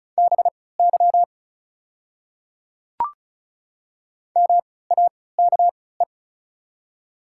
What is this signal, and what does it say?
Morse code, heard as a keyed tone, about 36 wpm, spaced as for 20 wpm, 700 Hz. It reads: BY MAKE
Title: 3x Character Spacing - Top 100 Words - 35wpm